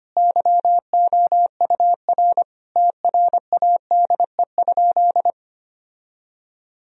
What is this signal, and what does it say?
Morse code, heard as a keyed tone, about 25 wpm, 700 Hz. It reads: YOUR TRADE?
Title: Sentences from Top 1000 Words - Review Part 4 - 25wpm